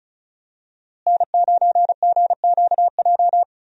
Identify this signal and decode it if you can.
Morse code, heard as a keyed tone, about 34 wpm, 700 Hz. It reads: N9GQJ
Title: USA Call Signs - 5 Characters - Repeated 3x - 35wpm